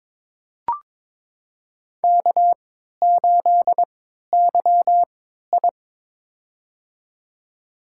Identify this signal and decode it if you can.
Morse code, heard as a keyed tone, about 22 wpm, spaced as for 12 wpm, 700 Hz. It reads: K8YI
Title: USA Call Signs - 4 Characters - 3x Character Spacing - 22wpm